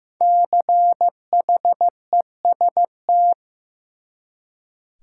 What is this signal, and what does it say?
Morse code, heard as a keyed tone, about 15 wpm, 700 Hz. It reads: CHEST
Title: Five Letter Words - 15wpm